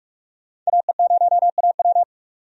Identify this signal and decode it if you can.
Morse code, heard as a keyed tone, about 46 wpm, 700 Hz. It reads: AE0AW
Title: USA Call Signs - 5 Characters - 45wpm